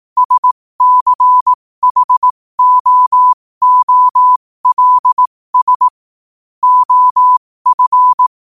Morse code, heard as a keyed tone, about 18 wpm, spaced as for 15 wpm, 1000 Hz. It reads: SCHOOLS OF
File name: qotdfiles / 15wpm